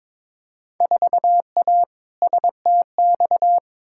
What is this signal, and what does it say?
Morse code, heard as a keyed tone, about 22 wpm, 700 Hz. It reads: 4A STX
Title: Field Day Exchanges - 22wpm